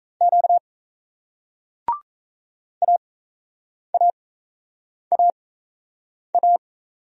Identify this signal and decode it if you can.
Morse code, heard as a keyed tone, about 39 wpm, spaced as for 8 wpm, 700 Hz. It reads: Q AAAA